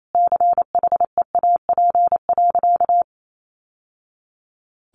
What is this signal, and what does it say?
Morse code, heard as a keyed tone, about 28 wpm, 700 Hz. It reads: CHEAP.
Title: Sentences from Top 1200 Words 28wpm